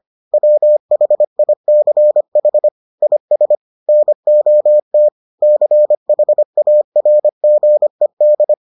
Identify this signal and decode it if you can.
Morse code, heard as a keyed tone, about 25 wpm, 600 Hz. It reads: WHICH IS NOT CHARGED